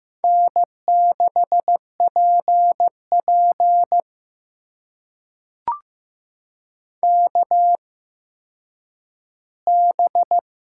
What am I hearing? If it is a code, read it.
Morse code, heard as a keyed tone, about 15 wpm, spaced as for 9 wpm, 700 Hz. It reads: N6PP KB
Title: USA Call Signs - 8x Character Spacing - 15wpm